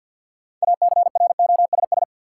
Morse code, heard as a keyed tone, about 48 wpm, 700 Hz. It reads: ACROSS
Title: Sentences from Top 700 Words - New Word Drill - 50wpm